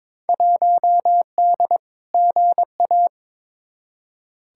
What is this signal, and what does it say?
Morse code, heard as a keyed tone, about 21 wpm, 700 Hz. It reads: 1D GA
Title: Field Day Exchanges - 22wpm